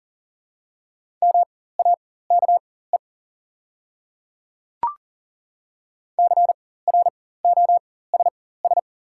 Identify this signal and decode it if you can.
Morse code, heard as a keyed tone, about 39 wpm, spaced as for 18 wpm, 700 Hz. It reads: MAKE CROSS
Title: 4x Character Spacing - Top 300 Words - 40wpm